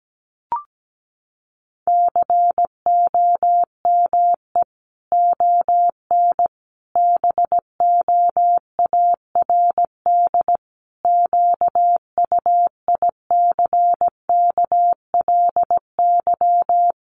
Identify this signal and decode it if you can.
Morse code, heard as a keyed tone, about 17 wpm, 700 Hz. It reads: COME ON BOARD QUICKLY
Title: Sentences from Top 1300 Words 17wpm